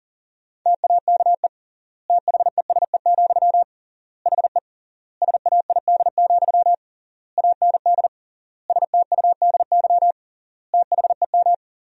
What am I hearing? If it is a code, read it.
Morse code, heard as a keyed tone, about 40 wpm, 700 Hz. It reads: TAKE THESE, HE SAID, AND STUDY THEM
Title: 3x Word-Spacing - Sentences from Top 300 Words - 40wpm